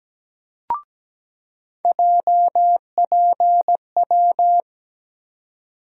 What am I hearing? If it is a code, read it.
Morse code, heard as a keyed tone, about 17 wpm, 700 Hz. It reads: JPW
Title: USA Call Sign Postfixes - 3 Letters 17wpm